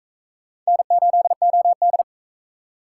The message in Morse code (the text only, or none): N8OD